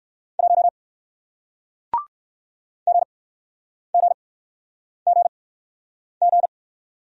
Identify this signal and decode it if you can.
Morse code, heard as a keyed tone, about 62 wpm, spaced as for 9 wpm, 700 Hz. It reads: 1 GGGG